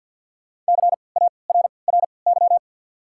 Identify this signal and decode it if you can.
Morse code, heard as a keyed tone, about 53 wpm, spaced as for 29 wpm, 690 Hz. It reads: CARRY
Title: 3x Character Spacing - Top 300 Words - 50wpm